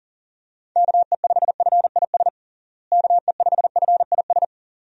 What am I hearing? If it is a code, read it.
Morse code, heard as a keyed tone, about 40 wpm, 700 Hz. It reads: KE5FIS KE5FIS
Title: USA Call Signs - 6 Characters - Repeated 2x - 40wpm